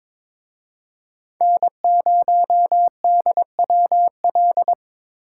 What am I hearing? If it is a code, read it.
Morse code, heard as a keyed tone, about 26 wpm, 700 Hz. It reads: N0DWL
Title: Speed Racing - USA 1x3 Call Signs 22wpm